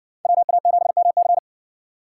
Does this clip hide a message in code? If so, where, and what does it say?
Yes: Morse code from 0.2 to 1.4 s, heard as a keyed tone, about 60 wpm, 700 Hz. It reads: WA7MC